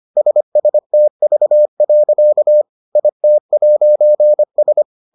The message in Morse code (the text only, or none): SSTV. IT'S